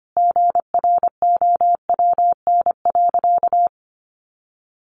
Morse code, heard as a keyed tone, about 25 wpm, 700 Hz. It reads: GROWN.
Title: Easy Phrases 25wpm